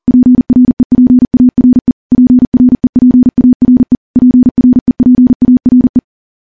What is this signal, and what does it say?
Morse code, heard as a keyed tone, about 40 wpm, 250 Hz. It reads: PREPARE PREPARE PREPARE
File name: morse / morse-training-40WPM